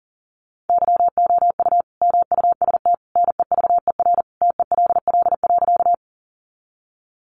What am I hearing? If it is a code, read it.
Morse code, heard as a keyed tone, about 40 wpm, 700 Hz. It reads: YOU MUST NEVER TELL.
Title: Sentences from Top 300 Words 40wpm